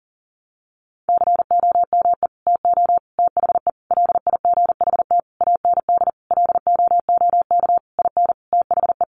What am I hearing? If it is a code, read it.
Morse code, heard as a keyed tone, about 40 wpm, 700 Hz. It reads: COME TO THE LIGHT AND LOOK IN THE
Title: Sentences from Top 1300 Words 40wpm